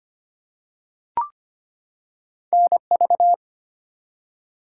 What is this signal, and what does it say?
Morse code, heard as a keyed tone, about 25 wpm, 700 Hz. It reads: NV
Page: morse-code-ninja-practice-25wpm